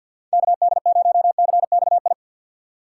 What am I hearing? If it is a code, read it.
Morse code, heard as a keyed tone, about 48 wpm, 700 Hz. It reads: KD0CXI
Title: USA Call Signs - 6x Character Spacing - 50wpm